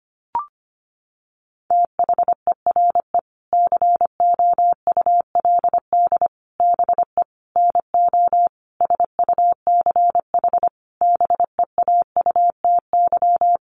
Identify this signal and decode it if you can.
Morse code, heard as a keyed tone, about 25 wpm, 700 Hz. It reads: THERE COULD BE NO SUCH BEAUTY
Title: Sentences from Top 600 Words 25wpm